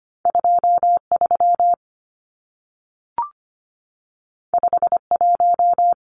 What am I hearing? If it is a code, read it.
Morse code, heard as a keyed tone, about 25 wpm, 700 Hz. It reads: 23 51
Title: Numbers - 2-Digit 25wpm